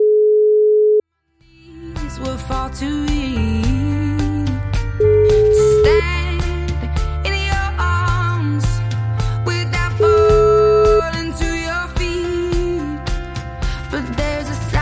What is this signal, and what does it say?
Ringback tone (Europe, 425 Hz).